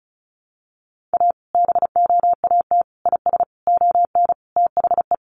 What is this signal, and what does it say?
Morse code, heard as a keyed tone, about 34 wpm, 700 Hz. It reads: A BOAT IS ON THE
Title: Sentences from Top 1500 Words 35wpm